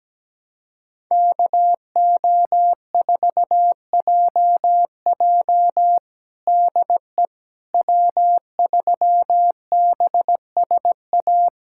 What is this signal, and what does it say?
Morse code, heard as a keyed tone, about 17 wpm, 700 Hz. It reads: KO4JJ DE W3BSA
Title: USA Callsign DE Callsign - 5 Characters - 17wpm